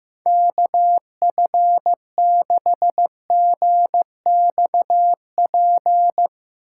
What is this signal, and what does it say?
Morse code, heard as a keyed tone, about 16 wpm, 700 Hz. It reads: KF6GXP
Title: Speed Racing - USA Call Signs 15wpm